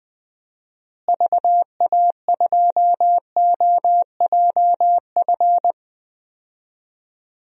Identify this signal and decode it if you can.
Morse code, heard as a keyed tone, about 20 wpm, 700 Hz. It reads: VA2OJF
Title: DXCC-100 Call Signs - 6 Characters - Repeated 3x - 20wpm